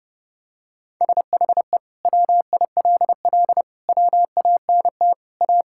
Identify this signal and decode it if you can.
Morse code, heard as a keyed tone, about 30 wpm, 700 Hz. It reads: SHE WILL WANT A
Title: Sentences from Top 1600 Words 30wpm